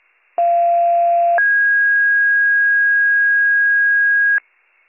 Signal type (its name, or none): two-tone page